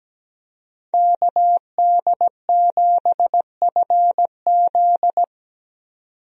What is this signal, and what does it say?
Morse code, heard as a keyed tone, about 17 wpm, 700 Hz. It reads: KD7FZ
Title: USA Call Signs - 5 Characters - 17wpm